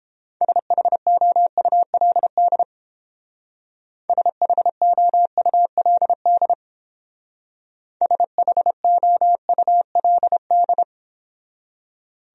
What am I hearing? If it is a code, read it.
Morse code, heard as a keyed tone, about 29 wpm, 700 Hz. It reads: SHOULD SHOULD SHOULD